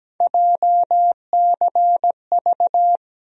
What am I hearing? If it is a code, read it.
Morse code, heard as a keyed tone, about 17 wpm, 700 Hz. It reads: JCV
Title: USA Call Sign Postfixes - 3 Letters 17wpm